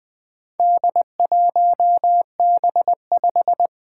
Morse code, heard as a keyed tone, about 20 wpm, 700 Hz. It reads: D1B5
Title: Four Challenging Letter-Number Combinations 20wpm